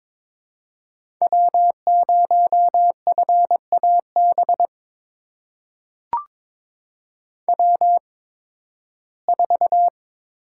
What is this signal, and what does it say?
Morse code, heard as a keyed tone, about 21 wpm, spaced as for 14 wpm, 700 Hz. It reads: W0FAB W4
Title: USA Call Signs - 5 Characters - 8x Character Spacing - 22wpm